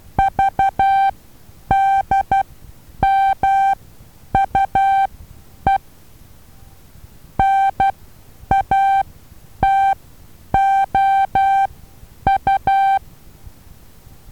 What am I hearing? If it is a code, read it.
Morse code, heard as a keyed tone, about 12 wpm, spaced as for 9 wpm, 780 Hz. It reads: VDMUE NATOU
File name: Taviro HA6VA B